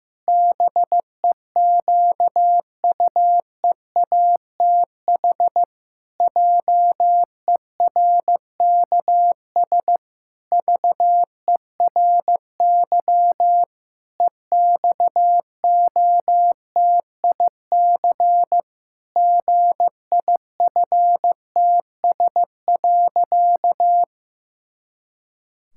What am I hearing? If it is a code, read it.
Morse code, heard as a keyed tone, about 14 wpm, 700 Hz. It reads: BEQUEATH JERKS VERY EXOTIC GIFTS.
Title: Pangram Challenge v2 - 15wpm